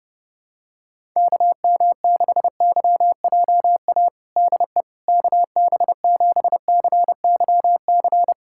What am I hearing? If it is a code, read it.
Morse code, heard as a keyed tone, about 30 wpm, 700 Hz. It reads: KM6YJA DE KB7CYC
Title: USA Callsign DE Callsign - 6 Characters - 30wpm